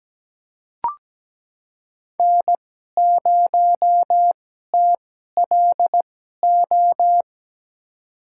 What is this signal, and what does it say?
Morse code, heard as a keyed tone, about 17 wpm, spaced as for 12 wpm, 700 Hz. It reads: N0TLO
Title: USA Call Signs - 2x Character Spacing - 17wpm